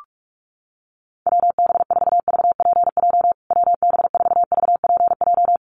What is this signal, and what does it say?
Morse code, heard as a keyed tone, about 44 wpm, 690 Hz. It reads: WB4VPJ WB4VPJ
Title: USA Call Signs Repeated 2x 45wpm